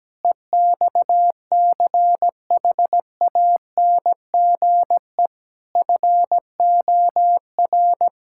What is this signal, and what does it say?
Morse code, heard as a keyed tone, about 17 wpm, 700 Hz. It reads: EXCHANGE FOR
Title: Sentences from Top 1000 Words - Review Part 6 - 17wpm